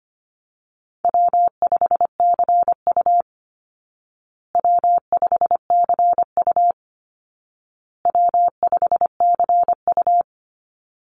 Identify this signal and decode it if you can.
Morse code, heard as a keyed tone, about 25 wpm, 700 Hz. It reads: W5CU W5CU W5CU
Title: Easy Call Signs Repeated 3x - 25wpm